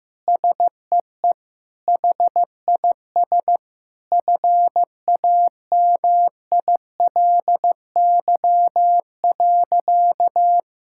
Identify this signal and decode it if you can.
Morse code, heard as a keyed tone, about 15 wpm, 700 Hz. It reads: SEE HIS FAMILY.
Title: Sentences from Top 1000 Words - Review Part 6 - 15wpm